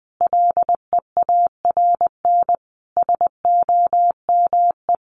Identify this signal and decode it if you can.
Morse code, heard as a keyed tone, about 20 wpm, 700 Hz. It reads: LEARN SOME
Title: Sentences from Top 900 Words 20wpm